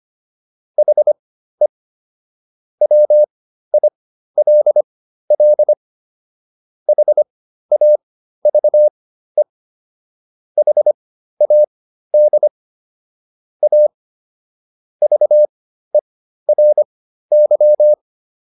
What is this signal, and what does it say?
Morse code, heard as a keyed tone, about 25 wpm, spaced as for 13 wpm, 600 Hz. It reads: HE WILL HAVE HAD A VERY